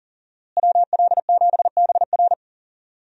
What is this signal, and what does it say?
Morse code, heard as a keyed tone, about 40 wpm, 700 Hz. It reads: WL7BR
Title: USA Call Signs - 5 Characters - Repeated 3x - 40wpm